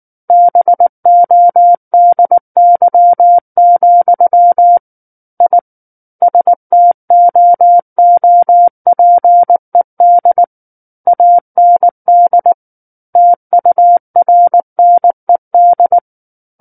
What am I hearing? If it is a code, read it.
Morse code, heard as a keyed tone, about 19 wpm, 700 Hz. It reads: BODY, I STOOPED AND TURNED
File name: Book One / 10-Chapter 10 - 19 WPM